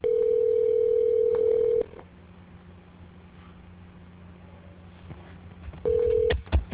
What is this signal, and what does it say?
Ringback tone (North America), 440 + 480 Hz.